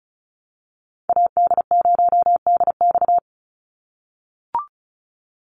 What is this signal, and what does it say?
Morse code, heard as a keyed tone, about 34 wpm, 700 Hz. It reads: AD0DX
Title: Call Signs - 35wpm